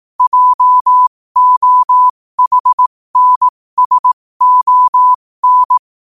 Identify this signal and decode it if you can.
Morse code, heard as a keyed tone, about 18 wpm, spaced as for 15 wpm, 1000 Hz. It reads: JOHNSON
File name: qotdfiles / 15wpm